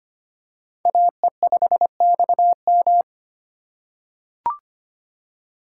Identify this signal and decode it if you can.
Morse code, heard as a keyed tone, about 25 wpm, 700 Hz. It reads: AE5XM
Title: USA Call Signs - 5 Characters - 4x Character Spacing - 25wpm